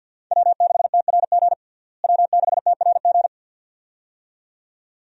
Morse code, heard as a keyed tone, about 51 wpm, 700 Hz. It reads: W6TRG W6TRG
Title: USA Call Signs - 5 Characters - Repeated 2x - 50wpm